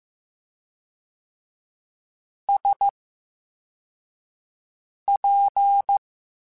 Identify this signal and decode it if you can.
Morse code, heard as a keyed tone, about 15 wpm, spaced as for 4 wpm, 800 Hz. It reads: SP
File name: Special INGs-mp3s / Special INGs-mp3s / VerySlowToFaster Ing @13